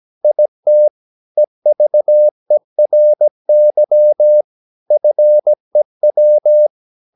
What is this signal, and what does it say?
Morse code, heard as a keyed tone, about 17 wpm, 600 Hz. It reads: IT EVERY FEW
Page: Romanen Never af Ken Follet på engelsk.